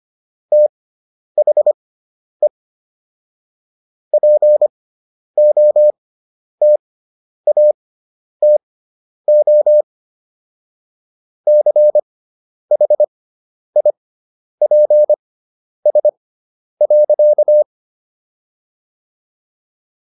Morse code, heard as a keyed tone, about 25 wpm, spaced as for 10 wpm, 600 Hz. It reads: THE POTATO CHIPS.